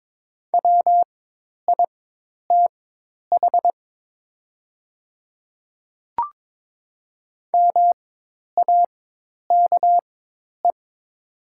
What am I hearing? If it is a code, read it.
Morse code, heard as a keyed tone, about 22 wpm, spaced as for 10 wpm, 700 Hz. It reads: WITH MAKE
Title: 4x Character Spacing - Top 100 Words - 22wpm